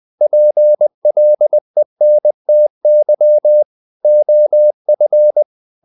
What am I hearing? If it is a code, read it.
Morse code, heard as a keyed tone, about 20 wpm, 600 Hz. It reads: PLENTY OF
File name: the old man and the sea / the old man and the sea part 1 20wpm